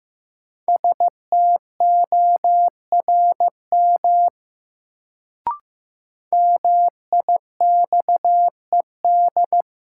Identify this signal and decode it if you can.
Morse code, heard as a keyed tone, about 15 wpm, 700 Hz. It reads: STORM MIXED
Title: Sentences from Top 1500 Words - New Word Drill - 15wpm